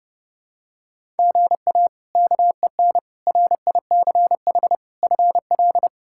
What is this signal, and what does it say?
Morse code, heard as a keyed tone, about 31 wpm, 700 Hz. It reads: GA KEN RICH FL
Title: K1USN SST QSO Practice - 30wpm